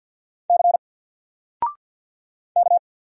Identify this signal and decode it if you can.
Morse code, heard as a keyed tone, about 52 wpm, 700 Hz. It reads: C K
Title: Single Letter-Number 50wpm